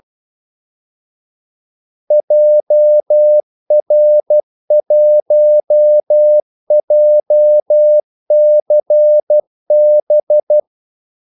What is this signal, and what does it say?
Morse code, heard as a keyed tone, about 12 wpm, 600 Hz. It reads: JR1JCB
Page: 2 ekstra ordmellemrum mellem kaldesignalerne